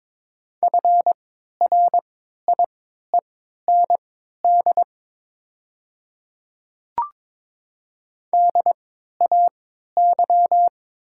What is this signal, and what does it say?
Morse code, heard as a keyed tone, about 22 wpm, spaced as for 13 wpm, 700 Hz. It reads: FRIEND DAY
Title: 3x Character Spacing - Top 300 Words - 22wpm